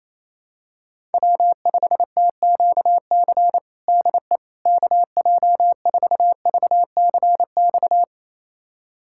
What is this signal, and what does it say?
Morse code, heard as a keyed tone, about 28 wpm, 700 Hz. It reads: W5TQC DE KJ4VCX
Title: USA Callsign DE Callsign 28wpm